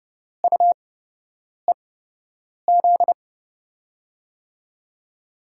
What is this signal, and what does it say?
Morse code, heard as a keyed tone, about 30 wpm, spaced as for 8 wpm, 700 Hz. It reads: UEZ
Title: USA Call Signs - 6 Characters - 8x Character Spacing - 30wpm